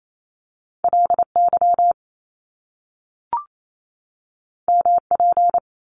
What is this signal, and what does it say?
Morse code, heard as a keyed tone, about 28 wpm, 700 Hz. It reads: LY MP
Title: Top Two Letter Combinations in Words 28wpm